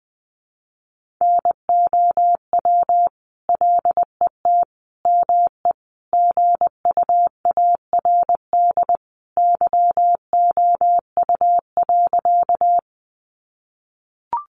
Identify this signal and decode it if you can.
Morse code, heard as a keyed tone, about 20 wpm, 700 Hz. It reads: NOW LET ME GUARD YOU.
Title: Sentences from Top 700 Words 20wpm